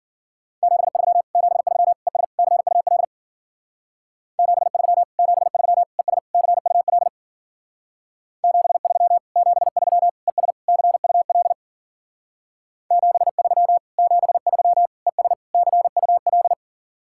Morse code, heard as a keyed tone, about 51 wpm, 700 Hz. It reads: 73 73 ES CUL 73 73 ES CUL 73 73 ES CUL 73 73 ES CUL